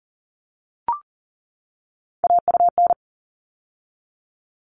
Morse code, heard as a keyed tone, about 40 wpm, 700 Hz. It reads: AUN
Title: All Cut Numbers - 3-Digit 40wpm